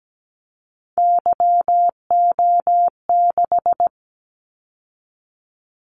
Morse code, heard as a keyed tone, about 17 wpm, 700 Hz. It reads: YO6
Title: Callsign Prefixes for Top 25 DXCC - 17wpm